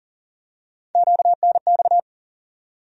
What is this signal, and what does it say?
Morse code, heard as a keyed tone, about 40 wpm, 700 Hz. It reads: QNX
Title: USA Call Sign Postfixes - 3 Letters 40wpm